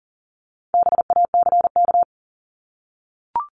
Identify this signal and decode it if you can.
Morse code, heard as a keyed tone, about 40 wpm, 700 Hz. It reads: BACK
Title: Top 300 Words Repeated 40wpm